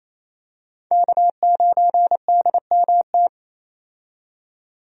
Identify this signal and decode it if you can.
Morse code, heard as a keyed tone, about 28 wpm, 700 Hz. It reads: K9DMT
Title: USA Call Signs - 5 Characters - 28wpm